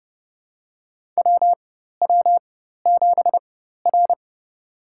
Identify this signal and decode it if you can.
Morse code, heard as a keyed tone, about 30 wpm, spaced as for 14 wpm, 700 Hz. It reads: WW7R
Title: USA Call Signs - 4 Characters - 4x Character Spacing - 30wpm